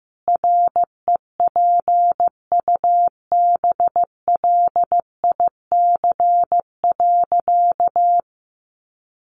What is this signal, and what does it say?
Morse code, heard as a keyed tone, about 15 wpm, 700 Hz. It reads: REPUBLIC.
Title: Sentences from Top 1500 Words 15wpm